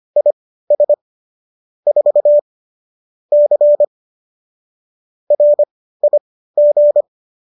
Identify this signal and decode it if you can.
Morse code, heard as a keyed tone, about 25 wpm, spaced as for 15 wpm, 590 Hz. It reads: IS 4 C RIG